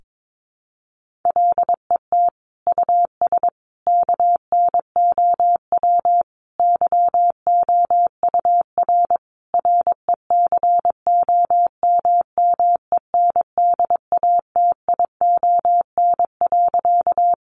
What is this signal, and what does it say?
Morse code, heard as a keyed tone, about 22 wpm, 700 Hz. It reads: LET US KNOW YOUR RECOMMENDATION.
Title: Sentences from Top 1400 Words 22wpm